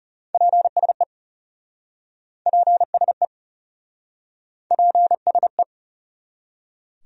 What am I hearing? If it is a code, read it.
Morse code, heard as a keyed tone, about 40 wpm, 700 Hz. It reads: PSE PSE PSE